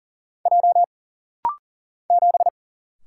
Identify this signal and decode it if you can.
Morse code, heard as a keyed tone, about 40 wpm, 700 Hz. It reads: J 7
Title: Single Letter-Number - Rapid-Fire 40wpm